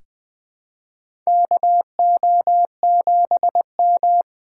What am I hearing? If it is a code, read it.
Morse code, heard as a keyed tone, about 21 wpm, 700 Hz. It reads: KO7M
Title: USA Call Signs - 4 Characters - 6x Character Spacing - 20wpm